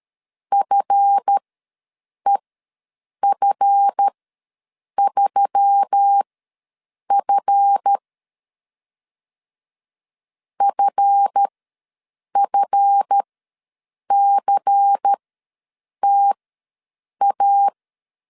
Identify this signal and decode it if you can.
Morse code, heard as a keyed tone, about 13 wpm, spaced as for 7 wpm, 800 Hz. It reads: FEF3F FFCTA
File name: LEZIONI / 7 GF9Z / 7 GF ws 12wpm 3140